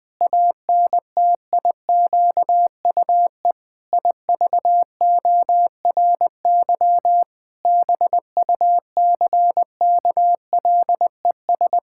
Morse code, heard as a keyed tone, about 20 wpm, 700 Hz. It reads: ANTIQUE IVORY BUCKLES
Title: Pangram Challenge v2 - 20wpm